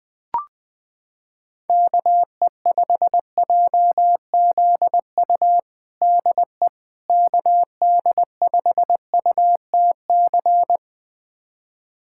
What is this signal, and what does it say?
Morse code, heard as a keyed tone, about 20 wpm, 700 Hz. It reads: KE5JZU DE KD5UTC
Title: USA Callsign DE Callsign 20wpm